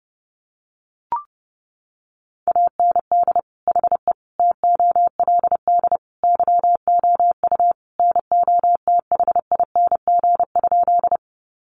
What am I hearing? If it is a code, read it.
Morse code, heard as a keyed tone, about 30 wpm, 700 Hz. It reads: AND HE TOLD YOU NOTHING?
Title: Sentences from Top 500 Words 30wpm